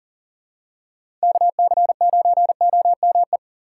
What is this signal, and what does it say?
Morse code, heard as a keyed tone, about 40 wpm, 700 Hz. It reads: KC9OME
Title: USA Call Signs - 2x Character Spacing - 40wpm